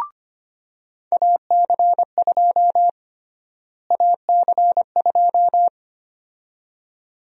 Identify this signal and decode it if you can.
Morse code, heard as a keyed tone, about 25 wpm, 700 Hz. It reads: AC2 AC2
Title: USA Call Sign Prefixes - Repeated 2x - 25wpm